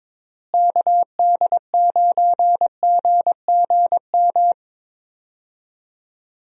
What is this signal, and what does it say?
Morse code, heard as a keyed tone, about 22 wpm, 700 Hz. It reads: KD9GGM
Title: USA Call Signs - 6 Characters - Repeated 3x - 22wpm